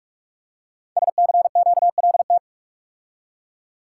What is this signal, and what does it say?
Morse code, heard as a keyed tone, about 44 wpm, 700 Hz. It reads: SCQLT
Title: Code Groups - 5 Letters - 45wpm